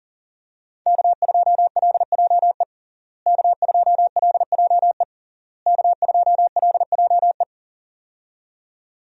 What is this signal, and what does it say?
Morse code, heard as a keyed tone, about 40 wpm, 700 Hz. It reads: K2LJE K2LJE K2LJE
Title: USA Call Signs - 5 Characters - Repeated 3x - 40wpm